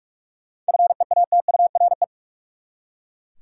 Morse code, heard as a keyed tone, about 41 wpm, 700 Hz. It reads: FEATURE
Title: Seven Letter Words - 45wpm